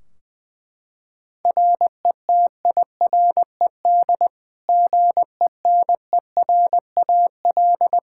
Morse code, heard as a keyed tone, about 20 wpm, 700 Hz. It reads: RETIRED GENERAL
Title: Practice QSO - Exchange 3 - More Info - 20wpm